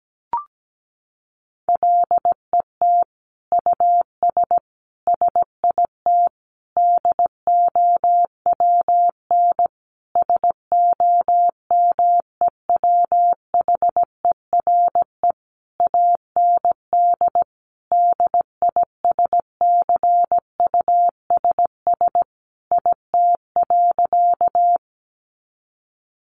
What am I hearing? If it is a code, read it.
Morse code, heard as a keyed tone, about 17 wpm, 700 Hz. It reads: LET US SIT DOWN SOMEWHERE AND DISCUSS IT.
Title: Sentences from Top 1400 Words 17wpm